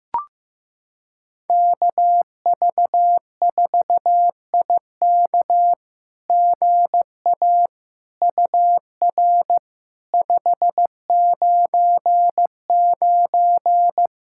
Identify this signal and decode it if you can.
Morse code, heard as a keyed tone, about 15 wpm, 700 Hz. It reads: KV4IK GA UR 599
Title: SOTA QSO Practice 15wpm